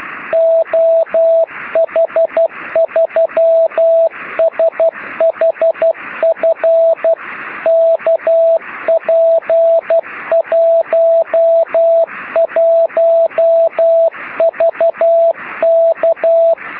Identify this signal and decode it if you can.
Morse code, heard as a keyed tone, about 12 wpm, 660 Hz, with radio noise. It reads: OH3SHF KP11VK
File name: sm7lcb / oh3shf 13cm